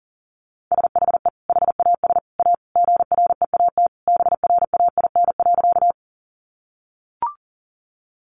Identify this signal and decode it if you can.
Morse code, heard as a keyed tone, about 37 wpm, 700 Hz. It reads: SHE HAS A GREAT BRAIN.
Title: Sentences from Top 1100 Words 40wpm